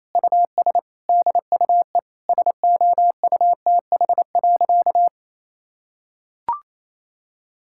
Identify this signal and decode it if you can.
Morse code, heard as a keyed tone, about 28 wpm, 700 Hz. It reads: US DUE SOUTH.
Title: Sentences from Top 1000 Words - Review Part 6 - 28wpm